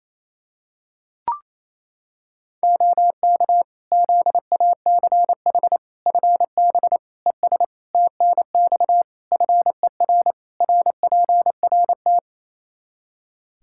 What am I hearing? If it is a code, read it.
Morse code, heard as a keyed tone, about 28 wpm, 700 Hz. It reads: OK ZACH FB ES TNX FER RPRT
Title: Practice QSO - Exchange 2 - Info - 28wpm